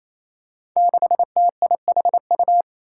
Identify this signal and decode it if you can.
Morse code, heard as a keyed tone, about 28 wpm, 700 Hz. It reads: 6TIHU
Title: Five Challenging Letter-Number Combinations 28wpm